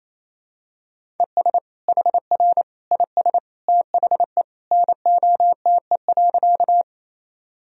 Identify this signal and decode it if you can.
Morse code, heard as a keyed tone, about 28 wpm, 700 Hz. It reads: ES HR IS THE NOTE.
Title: Sentences from Top 500 Words and Abbreviations 28wpm